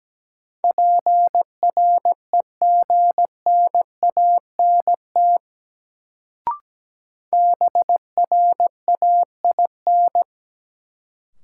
Morse code, heard as a keyed tone, about 17 wpm, 700 Hz. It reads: PREGNANT BRAIN
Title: Sentences from Top 1000 Words - Review Part 6 - New Word Drill - 17wpm